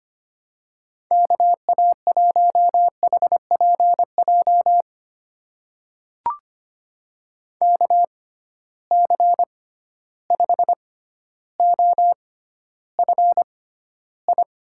Morse code, heard as a keyed tone, about 25 wpm, spaced as for 13 wpm, 700 Hz. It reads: KA1HPJ KC5OFI
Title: USA Call Signs - 6 Characters - 6x Character Spacing - 25wpm